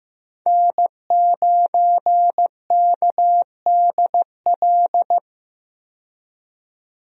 Morse code, heard as a keyed tone, about 16 wpm, 700 Hz. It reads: N9KDL
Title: USA Call Signs - 5 Characters - 15wpm